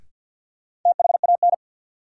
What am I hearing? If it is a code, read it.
Morse code, heard as a keyed tone, about 53 wpm, 700 Hz. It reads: THAN
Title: Sets of 2 Words - New Word Drill - 50wpm